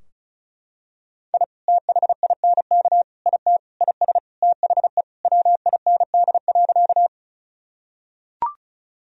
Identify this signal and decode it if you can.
Morse code, heard as a keyed tone, about 34 wpm, 700 Hz. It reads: I THINK IT IS THE WIND.
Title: Sentences from Top 1000 Words - Review Part 2 - 35wpm